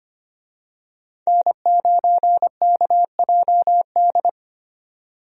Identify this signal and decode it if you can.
Morse code, heard as a keyed tone, about 25 wpm, 690 Hz. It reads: N9KJD
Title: USA Call Signs - 5 Characters - 25wpm